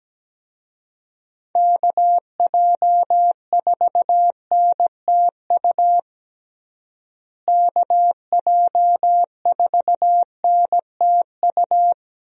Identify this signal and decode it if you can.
Morse code, heard as a keyed tone, about 17 wpm, 700 Hz. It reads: KJ4NTU KJ4NTU
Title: USA Call Signs - 6 Characters - Repeated 3x - 17wpm